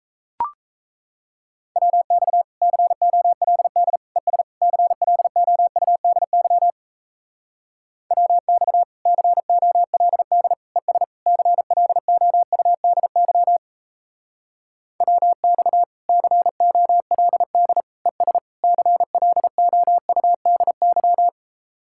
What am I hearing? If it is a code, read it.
Morse code, heard as a keyed tone, about 37 wpm, 700 Hz. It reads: WX COLD ES CLOUDY WX COLD ES CLOUDY WX COLD ES CLOUDY